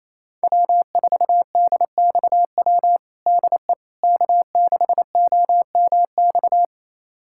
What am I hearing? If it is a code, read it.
Morse code, heard as a keyed tone, about 28 wpm, 700 Hz. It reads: W4DXW DE K6OMX
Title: USA Callsign DE Callsign - 5 Characters - 28wpm